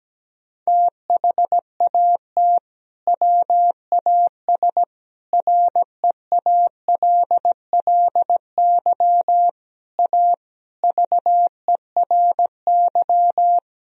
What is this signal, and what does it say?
Morse code, heard as a keyed tone, about 17 wpm, 700 Hz. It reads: THAT WAS REALLY A VERY
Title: Sentences from Top 1000 Words - Review Part 1 - 17wpm